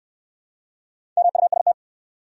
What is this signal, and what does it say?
Morse code, heard as a keyed tone, about 65 wpm, 700 Hz. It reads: MUST